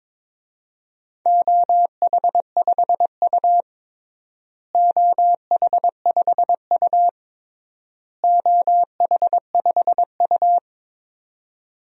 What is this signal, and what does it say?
Morse code, heard as a keyed tone, about 21 wpm, 700 Hz. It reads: OH5U OH5U OH5U
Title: DXCC-100 Call Signs - 4 Characters - Repeated 3x - 22wpm